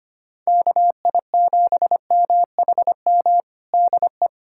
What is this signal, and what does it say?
Morse code, heard as a keyed tone, about 25 wpm, 700 Hz. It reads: KI7MHM DE
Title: USA Callsign DE Callsign - 6 Characters - 25wpm